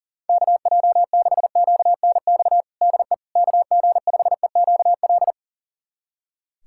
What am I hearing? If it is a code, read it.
Morse code, heard as a keyed tone, about 37 wpm, 700 Hz. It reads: KJ6QNX DE KG5EQL
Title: USA Callsign DE Callsign - 6 Characters - 40wpm